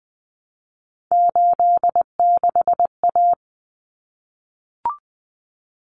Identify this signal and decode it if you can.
Morse code, heard as a keyed tone, about 20 wpm, 700 Hz. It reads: 86A
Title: Common Cut Numbers - 3-Digit 20wpm